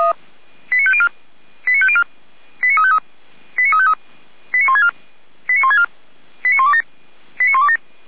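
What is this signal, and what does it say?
Five-tone selective call (ZVEI), heard as quick runs of five tones, tones of about 70 ms. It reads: 89504 89504 89353 89353 89164 89164 89127 89127
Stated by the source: ZVEI five-tone call received by SDR.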